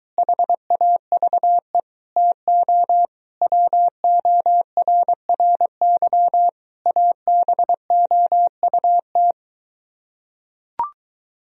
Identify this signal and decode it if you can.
Morse code, heard as a keyed tone, about 23 wpm, 700 Hz. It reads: HAVE TO WORRY ABOUT